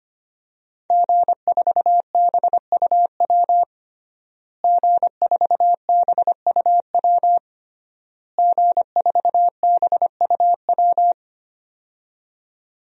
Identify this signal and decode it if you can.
Morse code, heard as a keyed tone, about 25 wpm, 700 Hz. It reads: G4BUW G4BUW G4BUW
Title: DXCC-100 Call Signs - 5 Characters - Repeated 3x - 25wpm